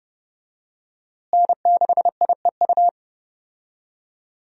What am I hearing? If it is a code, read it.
Morse code, heard as a keyed tone, about 31 wpm, 700 Hz. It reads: N6IEU
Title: USA Call Signs - 5 Characters - Repeated 2x - 30wpm